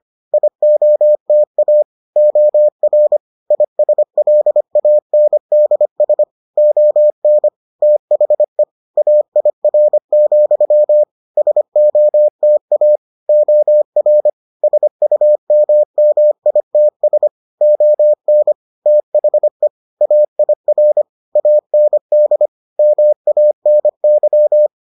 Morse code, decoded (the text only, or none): IOTA OR ISLANDS ON THE AIR, SOTA OR SUMMITS ON THE AIR AND MANY